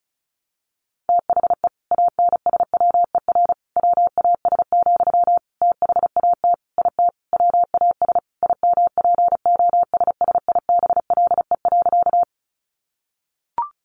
Morse code, heard as a keyed tone, about 34 wpm, 700 Hz. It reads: THE ANSWER WAS, THAT IT WAS IMPOSSIBLE.
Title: Sentences from Top 1100 Words 35wpm